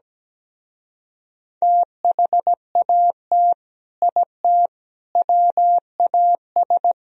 Morse code, heard as a keyed tone, about 17 wpm, 700 Hz. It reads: THAT IT WAS
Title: Speed-Racing - Sets of 4 Words 17wpm